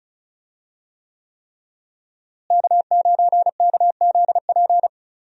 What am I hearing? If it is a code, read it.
Morse code, heard as a keyed tone, about 35 wpm, 700 Hz. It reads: K9KZP
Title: USA Call Signs - 5 Characters - Repeated 3x - 35wpm